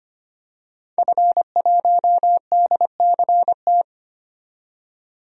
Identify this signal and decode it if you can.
Morse code, heard as a keyed tone, about 25 wpm, 690 Hz. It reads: F1DCT
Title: DXCC-100 Call Signs - 5 Characters - 25wpm